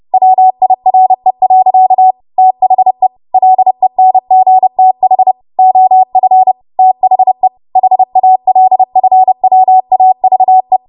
Morse code, heard as a keyed tone, about 30 wpm, 750 Hz. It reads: WIRE. THE LENGTH OF THE HALFWAVE
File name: Morse / 30WPM